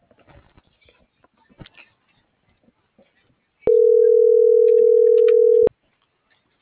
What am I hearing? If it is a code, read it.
Ringback tone (North America), 440 + 480 Hz.